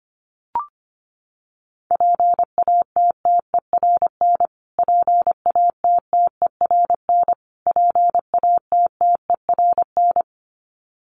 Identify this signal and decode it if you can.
Morse code, heard as a keyed tone, about 25 wpm, 700 Hz. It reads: PATTERN PATTERN PATTERN
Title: Top 500 Words Repeated 25wpm